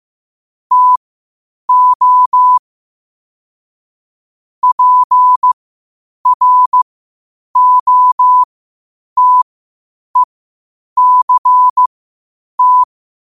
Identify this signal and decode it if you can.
Morse code, heard as a keyed tone, about 15 wpm, spaced as for 8 wpm, 1000 Hz. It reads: TO PROTECT
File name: qotdfiles / 8wpm